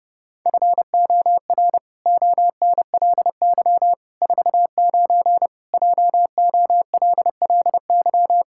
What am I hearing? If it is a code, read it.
Morse code, heard as a keyed tone, about 30 wpm, 700 Hz. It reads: FOR ONLY 49 JOLLY
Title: Pangram Challenge v2 - 30wpm